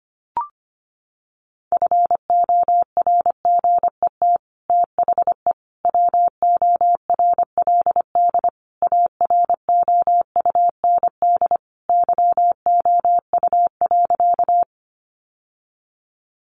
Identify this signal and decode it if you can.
Morse code, heard as a keyed tone, about 25 wpm, 700 Hz. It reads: FORGET THE WORLD AROUND YOU.
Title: Sentences from Top 1200 Words 25wpm